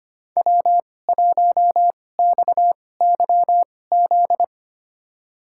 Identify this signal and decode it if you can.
Morse code, heard as a keyed tone, about 25 wpm, spaced as for 18 wpm, 700 Hz. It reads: W1XYZ
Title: USA Call Signs - 5 Characters - 2x Character Spacing - 25wpm